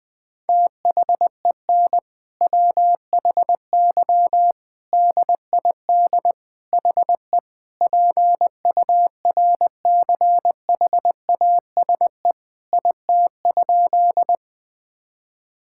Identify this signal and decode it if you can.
Morse code, heard as a keyed tone, about 20 wpm, 700 Hz. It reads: THEN WHY DID HE PURCHASE IT?
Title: Sentences from Top 1000 Words - Review Part 5 - 20wpm